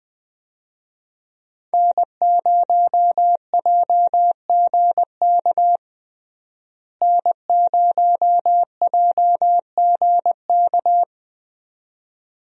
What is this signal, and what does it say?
Morse code, heard as a keyed tone, about 20 wpm, 700 Hz. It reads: N0JGK N0JGK
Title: USA Call Signs - 5 Characters - Repeated 2x - 20wpm